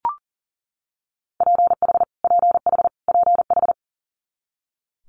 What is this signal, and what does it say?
Morse code, heard as a keyed tone, about 38 wpm, 700 Hz. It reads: PH PH PH
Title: Multi-letter Phonemes Repeated 3x 40wpm